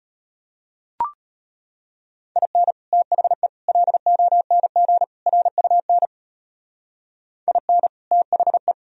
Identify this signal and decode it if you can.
Morse code, heard as a keyed tone, about 31 wpm, 700 Hz. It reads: IN THE LONG RUN IN THE
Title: Speed-Racing - Sets of 4 Words 25wpm